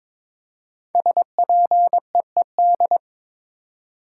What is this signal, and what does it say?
Morse code, heard as a keyed tone, about 22 wpm, 700 Hz. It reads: SPEED
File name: 5 Letter Words - First 3 Letters Given - 22wpm